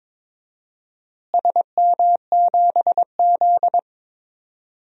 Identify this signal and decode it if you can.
Morse code, heard as a keyed tone, about 22 wpm, 700 Hz. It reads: SM7Z
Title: DXCC-100 Call Signs - 4 Characters - Repeated 3x - 22wpm